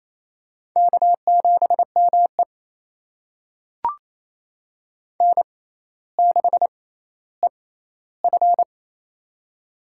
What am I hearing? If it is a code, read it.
Morse code, heard as a keyed tone, about 28 wpm, spaced as for 14 wpm, 700 Hz. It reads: K7ME N6EF
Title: USA Call Signs - 4 Characters - 6x Character Spacing - 28wpm